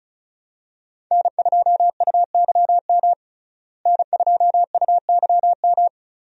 Morse code, heard as a keyed tone, about 34 wpm, 700 Hz. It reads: N2UYM N2UYM
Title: USA Call Signs - 5 Characters - Repeated 2x - 35wpm